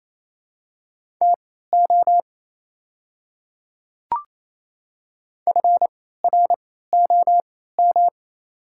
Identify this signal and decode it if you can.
Morse code, heard as a keyed tone, about 28 wpm, spaced as for 16 wpm, 700 Hz. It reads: TO FROM
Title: 3x Character Spacing - Top 300 Words - 28wpm